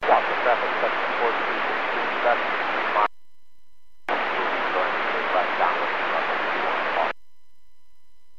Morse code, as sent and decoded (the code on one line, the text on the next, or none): none